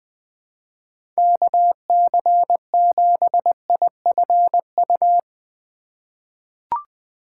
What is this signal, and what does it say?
Morse code, heard as a keyed tone, about 20 wpm, 700 Hz. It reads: KC7IFU
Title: USA Call Signs - 6 Characters - 6x Character Spacing - 20wpm